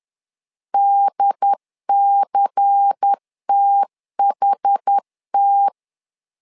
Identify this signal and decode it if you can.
Morse code, heard as a keyed tone, about 10 wpm, 800 Hz. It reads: DCTHT
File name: LEZIONI / 5 LBVC / 5 LBVC 10wpm 2349